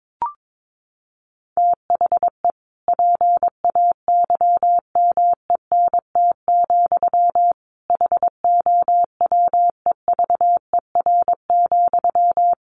Morse code, heard as a keyed tone, about 22 wpm, 700 Hz. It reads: THE PAYMENT, HOWEVER,
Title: Sentences from Top 1200 Words 22wpm